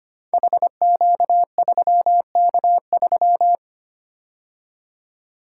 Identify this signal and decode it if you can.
Morse code, heard as a keyed tone, about 25 wpm, 700 Hz. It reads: HQ3K3
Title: Code Groups - 5 Letter:Numbers - 25wpm